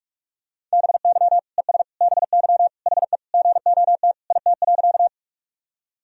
Morse code, heard as a keyed tone, about 46 wpm, 700 Hz. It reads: BY ES BY HE GOT IT.
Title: Sentences from Top 300 Words and Abbreviations 45wpm